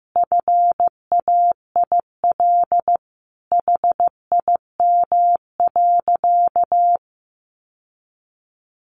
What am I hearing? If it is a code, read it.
Morse code, heard as a keyed tone, about 15 wpm, 700 Hz. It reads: FAIL HIM.
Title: Sentences from Top 1100 Words 15wpm